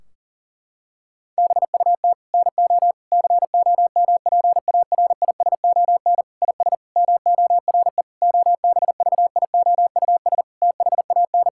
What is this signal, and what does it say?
Morse code, heard as a keyed tone, about 40 wpm, 700 Hz. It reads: BUT NO COMPARISON IS MORE OBVIOUS THAN
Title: Sentences from Top 1600 Words 40wpm